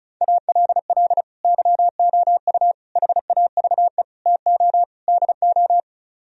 Morse code, heard as a keyed tone, about 45 wpm, 700 Hz. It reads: ALL YOU HAVE TO DO